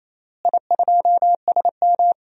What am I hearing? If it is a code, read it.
Morse code, heard as a keyed tone, about 28 wpm, 700 Hz. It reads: I2SM
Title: DXCC-100 Call Signs - 4 Characters - Repeated 3x - 28wpm